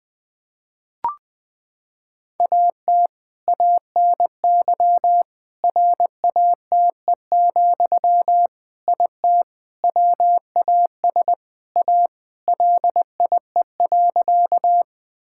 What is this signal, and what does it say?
Morse code, heard as a keyed tone, about 20 wpm, 700 Hz. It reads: AT ANY RATE, IT WAS A LIE.
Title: Sentences from Top 1600 Words 20wpm